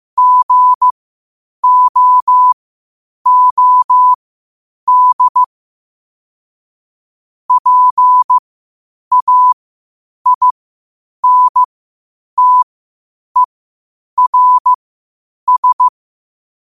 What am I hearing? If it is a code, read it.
Morse code, heard as a keyed tone, about 15 wpm, spaced as for 8 wpm, 1000 Hz. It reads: GOOD PAINTERS
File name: qotdfiles / 8wpm